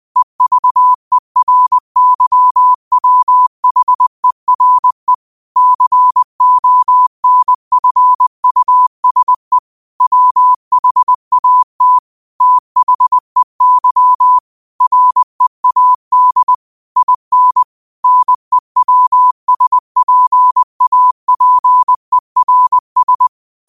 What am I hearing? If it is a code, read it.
Morse code, heard as a keyed tone, about 20 wpm, 1000 Hz. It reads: EVERYWHERE CONFUSE WHAT THEY READ IN NEWSPAPERS
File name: qotdfiles / 20wpm